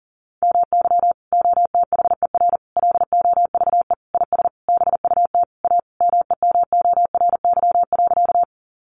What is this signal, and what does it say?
Morse code, heard as a keyed tone, about 40 wpm, 700 Hz. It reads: MY OTHER LOVE IS BUT A MEMORY.
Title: Sentences from Top 1300 Words 40wpm